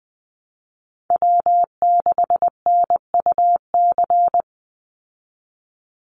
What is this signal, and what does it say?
Morse code, heard as a keyed tone, about 20 wpm, 700 Hz. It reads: W6NUC
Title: Call Signs Repeated 2x - 20wpm